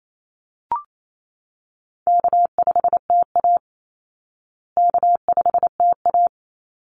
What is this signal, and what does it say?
Morse code, heard as a keyed tone, about 28 wpm, 700 Hz. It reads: K5TA K5TA
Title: Easy Call Signs Repeated 2x - 28wpm